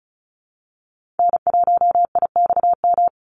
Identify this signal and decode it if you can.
Morse code, heard as a keyed tone, about 34 wpm, 700 Hz. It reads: N1IXM
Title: USA Call Signs 35wpm